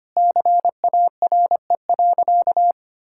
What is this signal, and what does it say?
Morse code, heard as a keyed tone, about 25 wpm, 700 Hz. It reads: CARE.
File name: practice / 25wpm / Sentences with 9-Word Phrases - 25wpm